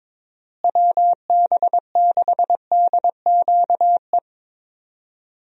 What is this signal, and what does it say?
Morse code, heard as a keyed tone, about 22 wpm, 700 Hz. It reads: WB6DQE
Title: USA Call Signs - 6 Characters - Repeated 2x - 22wpm